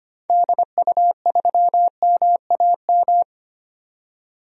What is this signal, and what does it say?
Morse code, heard as a keyed tone, about 25 wpm, 700 Hz. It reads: DU3MAM
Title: DXCC-100 Call Signs - 6 Characters - 25wpm